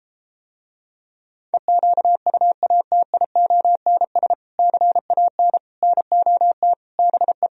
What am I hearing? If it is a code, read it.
Morse code, heard as a keyed tone, about 33 wpm, 700 Hz. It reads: EQUATIONS CAN NOT BE